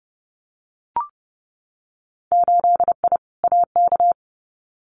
Morse code, heard as a keyed tone, about 30 wpm, 700 Hz. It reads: 8I AK
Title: Winter Field Day Exchanges Encore 30wpm